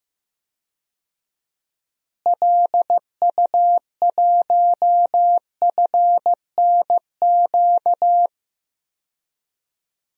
Morse code, heard as a keyed tone, about 16 wpm, 700 Hz. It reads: LU1FNQ
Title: DXCC-100 Call Signs - 6 Characters - Repeated 2x - 15wpm